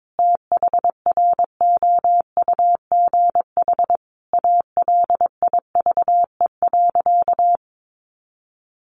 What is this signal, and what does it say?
Morse code, heard as a keyed tone, about 22 wpm, 700 Hz. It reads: THROUGH ALIVE.
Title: A Sound Of Thunder 22wpm